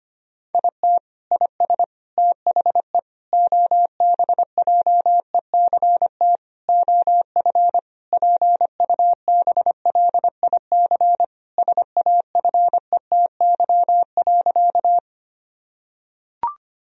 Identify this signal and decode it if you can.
Morse code, heard as a keyed tone, about 25 wpm, 700 Hz. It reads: IT IS THE OBJECT OF PUBLIC SAFETY.
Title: Sentences from Top 1000 Words - Review Part 4 - 25wpm